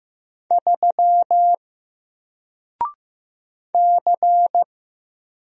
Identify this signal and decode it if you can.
Morse code, heard as a keyed tone, about 15 wpm, 700 Hz. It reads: 3 C
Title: Single Letter-Number 15wpm